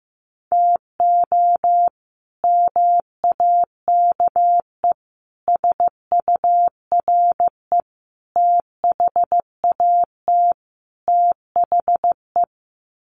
Morse code, heard as a keyed tone, about 15 wpm, 700 Hz. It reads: TO MAKE SURE THAT THE
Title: Sets of 5 Words 15wpm